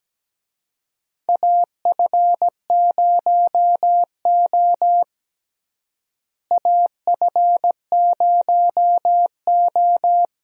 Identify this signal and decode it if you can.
Morse code, heard as a keyed tone, about 17 wpm, 700 Hz. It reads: AF0O AF0O
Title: USA Call Signs - 4 Characters - Repeated 3x - 17wpm